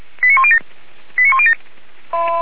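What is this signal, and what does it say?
Five-tone selective call (ZVEI), heard as quick runs of five tones, tones of about 70 ms. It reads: 89107 89107
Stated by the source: ZVEI five-tone call received by SDR.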